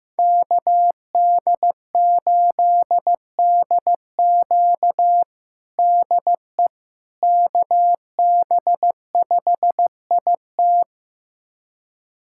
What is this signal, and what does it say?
Morse code, heard as a keyed tone, about 16 wpm, 700 Hz. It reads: KD8DQ DE KB5IT
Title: USA Callsign DE Callsign - 5 Characters - 15wpm